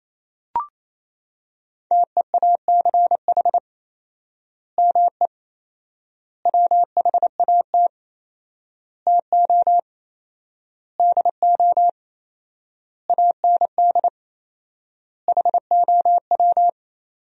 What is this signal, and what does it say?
Morse code, heard as a keyed tone, about 28 wpm, 700 Hz. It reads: TEACH ME WHAT TO DO AND HOW
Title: 4x Word-Spacing - Sentences from Top 500 Words - 28wpm